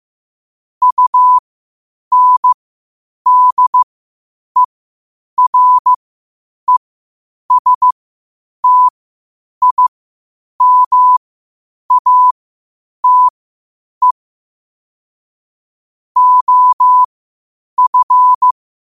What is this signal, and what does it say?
Morse code, heard as a keyed tone, about 15 wpm, spaced as for 8 wpm, 1000 Hz. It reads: UNDERESTIMATE OF
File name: qotdfiles / 8wpm